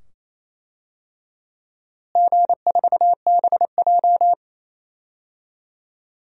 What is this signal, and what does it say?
Morse code, heard as a keyed tone, about 28 wpm, 700 Hz. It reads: G4BJ
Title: DXCC-100 Call Signs - 4 Characters - Repeated 2x - 28wpm